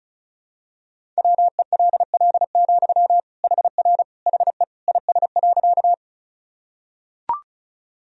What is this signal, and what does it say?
Morse code, heard as a keyed tone, about 34 wpm, 700 Hz. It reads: WELL, HR HE IS.
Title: Sentences from Top 200 Words and Abbreviations 35wpm